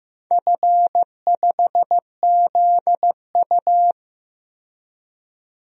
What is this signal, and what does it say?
Morse code, heard as a keyed tone, about 15 wpm, 700 Hz. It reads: F5ZU
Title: DXCC-100 Call Signs - 4 Characters - 15wpm